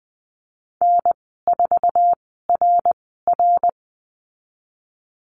Morse code, heard as a keyed tone, about 20 wpm, spaced as for 15 wpm, 700 Hz. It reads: N4RR
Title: 055 Lesson 10g - Letter r - Call Signs 20wpm